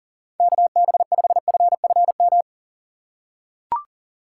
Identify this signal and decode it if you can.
Morse code, heard as a keyed tone, about 40 wpm, 700 Hz. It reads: KB5FFM
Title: USA Call Signs - 4x Character Spacing - 40wpm